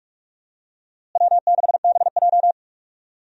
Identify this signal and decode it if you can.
Morse code, heard as a keyed tone, about 47 wpm, 700 Hz. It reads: W6BJ
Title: USA Call Signs - 4 Characters - Repeated 2x - 45wpm